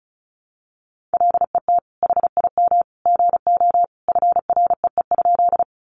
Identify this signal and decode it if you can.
Morse code, heard as a keyed tone, about 34 wpm, 700 Hz. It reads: LET HIM GO FREE?
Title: Sentences from Top 500 Words 35wpm